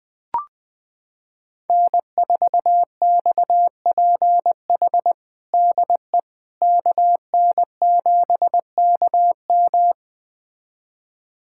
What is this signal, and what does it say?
Morse code, heard as a keyed tone, about 21 wpm, 700 Hz. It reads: N4XPH DE KN7KM
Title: USA Callsign DE Callsign - 5 Characters - 20wpm